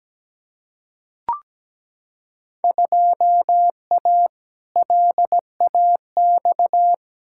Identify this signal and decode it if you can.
Morse code, heard as a keyed tone, about 17 wpm, 700 Hz. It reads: 2A LAX
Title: Field Day Exchanges - 17wpm